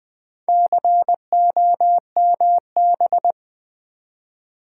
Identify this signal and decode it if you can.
Morse code, heard as a keyed tone, about 20 wpm, 700 Hz. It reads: COMB
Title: Build A Word - 4 Letters At A Time - 11-Letter Words - 20wpm